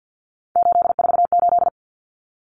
Morse code, heard as a keyed tone, about 50 wpm, 700 Hz. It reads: 848
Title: Common Cut Numbers - 3-Digit 50wpm